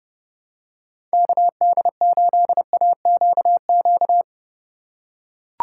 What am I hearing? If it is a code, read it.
Morse code, heard as a keyed tone, about 30 wpm, 700 Hz. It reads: KD8AQQ
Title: USA Call Signs - 6 Characters - 3x Character Spacing - 30wpm